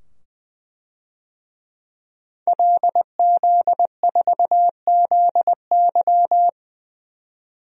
Morse code, heard as a keyed tone, about 21 wpm, 700 Hz. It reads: LZ4ZY
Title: DXCC-100 Call Signs - 5 Characters - 20wpm